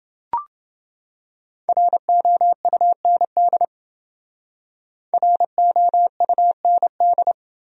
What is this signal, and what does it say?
Morse code, heard as a keyed tone, about 28 wpm, 700 Hz. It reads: ROUND ROUND